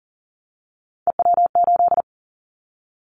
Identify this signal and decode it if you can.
Morse code, heard as a keyed tone, about 38 wpm, 700 Hz. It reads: EW8
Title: Callsign Prefixes for Top 50 DXCC - 40wpm